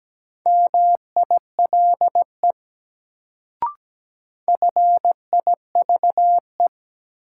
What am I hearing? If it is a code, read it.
Morse code, heard as a keyed tone, about 17 wpm, 700 Hz. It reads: MILE FIVE
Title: Sentences from Top 400 Words - New Word Drill - 17wpm